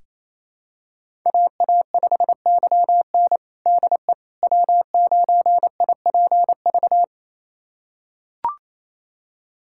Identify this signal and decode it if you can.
Morse code, heard as a keyed tone, about 28 wpm, 700 Hz. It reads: AA5YN DE W9IPV
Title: USA Callsign DE Callsign - 5 Characters - 28wpm